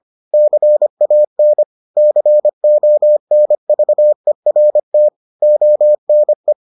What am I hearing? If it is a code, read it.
Morse code, heard as a keyed tone, about 25 wpm, 600 Hz. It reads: CAN CONVERT ONE